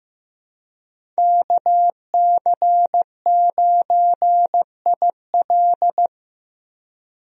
Morse code, heard as a keyed tone, about 16 wpm, 700 Hz. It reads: KC9IL
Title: USA Call Signs - 5 Characters - Repeated 2x - 15wpm